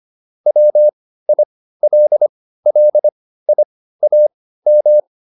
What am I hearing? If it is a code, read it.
Morse code, heard as a keyed tone, about 25 wpm, spaced as for 15 wpm, 600 Hz. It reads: WILLIAM